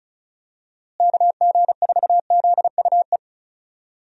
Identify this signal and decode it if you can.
Morse code, heard as a keyed tone, about 36 wpm, 700 Hz. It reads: KG4ZUE
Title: USA Call Signs - 6 Characters - Repeated 2x - 35wpm